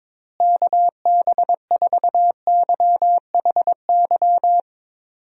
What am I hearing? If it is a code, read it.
Morse code, heard as a keyed tone, about 22 wpm, 700 Hz. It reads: KB4YHY
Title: USA Call Signs - 6 Characters - Repeated 3x - 22wpm